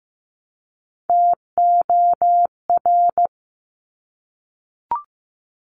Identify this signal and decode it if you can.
Morse code, heard as a keyed tone, about 15 wpm, 700 Hz. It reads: TOR
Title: Top Three Letter Combinations in Words 15wpm